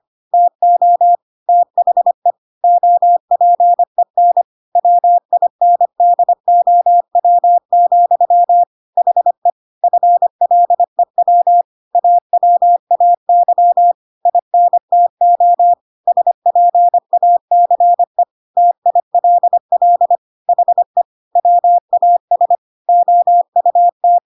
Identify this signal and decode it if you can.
Morse code, heard as a keyed tone, about 25 wpm, 700 Hz. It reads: TO THE OPEN WINDOW, HE FLEW AWAY INTO SPACE TILL HE WAS OUT
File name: Pinocchio / Pinocchio ch5 25 25 ik1hge